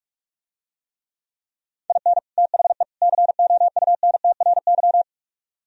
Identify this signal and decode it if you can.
Morse code, heard as a keyed tone, about 47 wpm, 700 Hz. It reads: IN THE COUNTRY